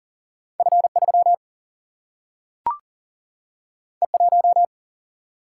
Morse code, heard as a keyed tone, about 40 wpm, 700 Hz. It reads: F 3 E 1
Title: Single-Letter Single-Number 40wpm